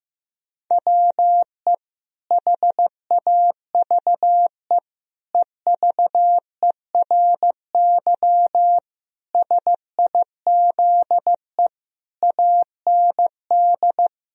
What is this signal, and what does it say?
Morse code, heard as a keyed tone, about 15 wpm, 700 Hz. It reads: WE HAVE EVERY SIZE AND
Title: Sentences from Top 1000 Words - Review Part 2 - 15wpm